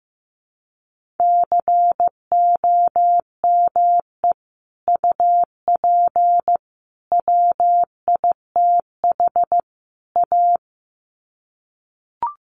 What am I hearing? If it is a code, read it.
Morse code, heard as a keyed tone, about 15 wpm, 700 Hz. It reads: COME UP WITH A
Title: Sets of 4 Words 15wpm